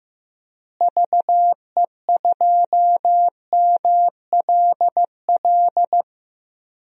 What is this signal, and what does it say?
Morse code, heard as a keyed tone, about 15 wpm, 700 Hz. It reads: VE2MLL
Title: DXCC-100 Call Signs - 6 Characters - Repeated 2x - 15wpm